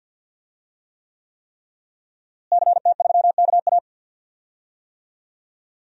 Morse code, heard as a keyed tone, about 51 wpm, 700 Hz. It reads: CT3KA
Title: DXCC-100 Call Signs - 5 Characters - Repeated 3x - 50wpm